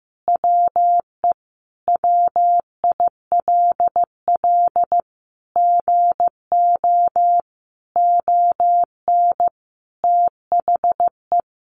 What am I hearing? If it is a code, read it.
Morse code, heard as a keyed tone, about 15 wpm, 700 Hz. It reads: WE WILL GO ON THE
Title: Sentences from Top 1400 Words 15wpm